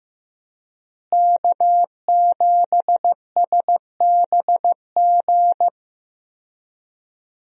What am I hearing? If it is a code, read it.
Morse code, heard as a keyed tone, about 15 wpm, 700 Hz. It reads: K7SBG
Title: USA Call Signs - 5 Characters - Repeated 2x - 15wpm